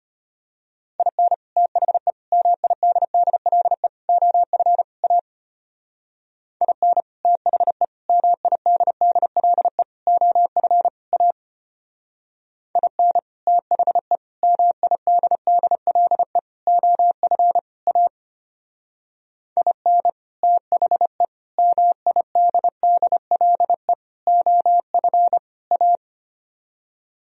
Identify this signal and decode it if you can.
Morse code, heard as a keyed tone, about 33 wpm, 700 Hz. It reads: IN THE MIDDLE OF A IN THE MIDDLE OF A IN THE MIDDLE OF A IN THE MIDDLE OF A